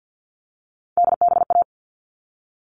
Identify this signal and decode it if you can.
Morse code, heard as a keyed tone, about 50 wpm, 700 Hz. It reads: DBA
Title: All Cut Numbers - 3-Digit 50wpm